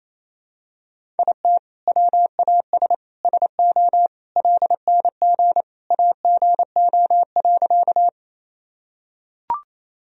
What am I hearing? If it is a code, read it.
Morse code, heard as a keyed tone, about 28 wpm, 700 Hz. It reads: IT WAS SO LNG AGO.
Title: Sentences from Top 500 Words and Abbreviations 28wpm